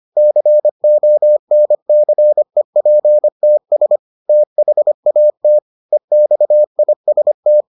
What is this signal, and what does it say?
Morse code, heard as a keyed tone, about 25 wpm, 600 Hz. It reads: CONCEPTS THAT EXIST